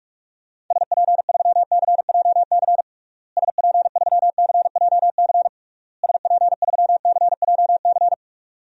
Morse code, heard as a keyed tone, about 46 wpm, 700 Hz. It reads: SP3CJC SP3CJC SP3CJC
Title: DXCC-100 Call Signs - 6 Characters - Repeated 3x - 45wpm